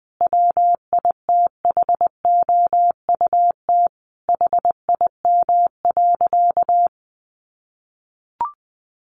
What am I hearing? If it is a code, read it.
Morse code, heard as a keyed tone, about 20 wpm, 700 Hz. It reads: WITHOUT HIM.
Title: Sentences from Top 1400 Words 20wpm